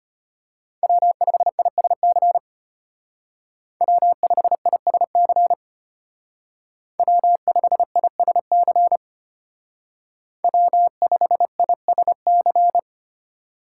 Morse code, heard as a keyed tone, about 32 wpm, 700 Hz. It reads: W5ISC W5ISC W5ISC W5ISC